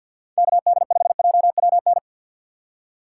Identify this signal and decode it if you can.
Morse code, heard as a keyed tone, about 62 wpm, 700 Hz. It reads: KD5JWN
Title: Speed Racing - USA 2x3 Call Signs 50wpm